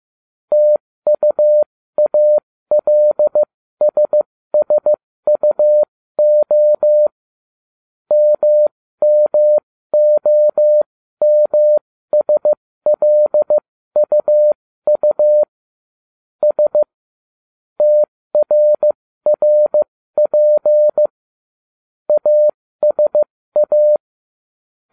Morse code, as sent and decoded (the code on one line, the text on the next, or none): - ..- .- .-.. ... ... ..- --- / -- -- --- -- ... .-.. ..- ..- / ... / - .-. .-. .--. / .- ... .-
TUALSSUO MMOMSLUU S TRRP ASA